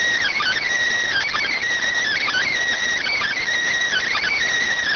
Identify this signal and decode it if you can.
Emergency Alert System (SAME) tones, heard as harsh buzzing bursts. It is heard from 0.2 to 4.2 s.